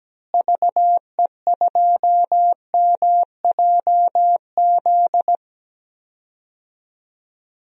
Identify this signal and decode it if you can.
Morse code, heard as a keyed tone, about 17 wpm, 700 Hz. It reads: VE2MJZ
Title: DXCC-100 Call Signs - 6 Characters - Repeated 2x - 17wpm